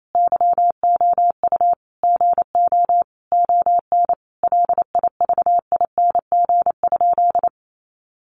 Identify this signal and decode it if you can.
Morse code, heard as a keyed tone, about 28 wpm, 700 Hz. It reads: YOU GO ON LIVING?
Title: Sentences from Top 1100 Words 28wpm